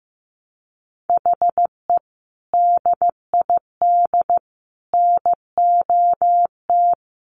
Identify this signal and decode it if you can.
Morse code, heard as a keyed tone, about 15 wpm, 700 Hz. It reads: HE DID NOT
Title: Sets of 4 Words 15wpm